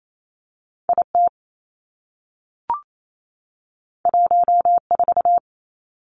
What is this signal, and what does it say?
Morse code, heard as a keyed tone, about 28 wpm, 700 Hz. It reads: I T 1 4
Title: Two Challenging Letter-Number Combinations 28wpm